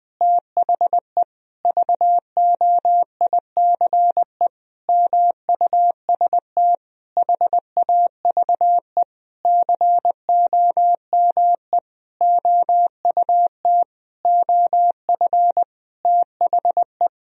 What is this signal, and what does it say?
Morse code, heard as a keyed tone, about 21 wpm, 700 Hz. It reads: THE VOICE MUST HAVE COME OUT OF THE
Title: Sentences from Top 1000 Words - Review Part 2 - 20wpm